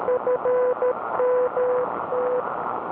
Morse code, heard as a keyed tone, about 13 wpm, 500 Hz, with radio noise. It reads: FMT
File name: w6oqi-oct-21-09-80m-shack-snip